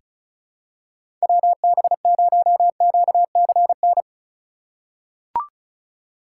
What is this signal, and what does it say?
Morse code, heard as a keyed tone, about 34 wpm, 700 Hz. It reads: WB0QCN
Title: USA Call Signs - 6 Characters - 4x Character Spacing - 35wpm